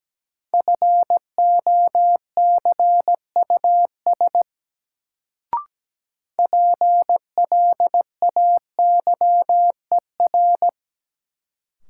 Morse code, heard as a keyed tone, about 17 wpm, 700 Hz. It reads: FOCUS PLAYER
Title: Sentences from Top 1000 Words - Review Part 4 - New Word Drill - 17wpm